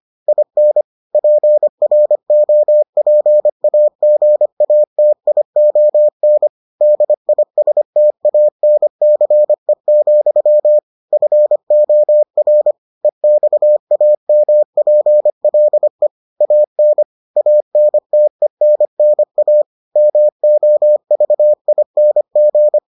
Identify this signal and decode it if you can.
Morse code, heard as a keyed tone, about 25 wpm, 600 Hz. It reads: IN PROPAGATION DISTANCE, FOR EXAMPLE AN ANTENNA MOVING